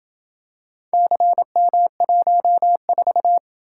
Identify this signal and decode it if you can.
Morse code, heard as a keyed tone, about 28 wpm, 700 Hz. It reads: CM14